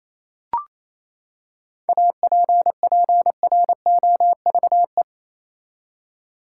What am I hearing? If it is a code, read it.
Morse code, heard as a keyed tone, about 28 wpm, 700 Hz. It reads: APPROVE
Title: Seven Letter Words - 28wpm